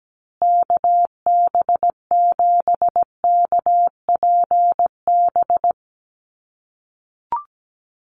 Morse code, heard as a keyed tone, about 17 wpm, 700 Hz. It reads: KB7KPB
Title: USA Call Signs Repeated 3x 17wpm